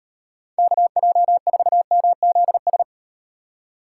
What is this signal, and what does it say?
Morse code, heard as a keyed tone, about 38 wpm, 700 Hz. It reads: KJ4MZS